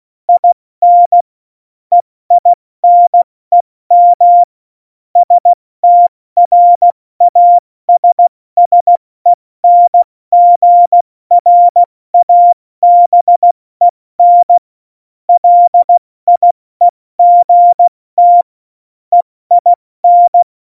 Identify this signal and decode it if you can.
Morse code, heard as a keyed tone, about 16 wpm, spaced as for 14 wpm, 700 Hz. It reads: IN EINEM STRASSENGRABEN LIEGT EIN
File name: Morsekurs / Witze / Witz 05. Februar 2019, 70 F80